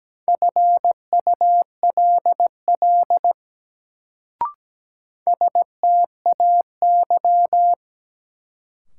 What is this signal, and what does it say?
Morse code, heard as a keyed tone, about 17 wpm, 700 Hz. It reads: FULL STAY
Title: Sentences from Top 500 Words - New Word Drill - 17wpm